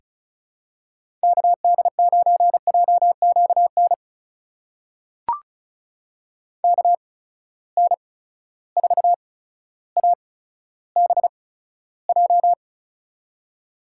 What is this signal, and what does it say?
Morse code, heard as a keyed tone, about 34 wpm, spaced as for 15 wpm, 700 Hz. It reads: KD9JQN KN4ABJ
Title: USA Call Signs - 6 Characters - 8x Character Spacing - 35wpm